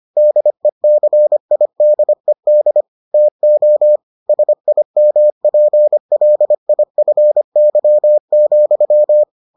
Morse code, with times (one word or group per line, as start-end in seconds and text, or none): 0.2-2.8 DECIDED
3.1-4.0 TO
4.3-9.2 SIMPLIFY,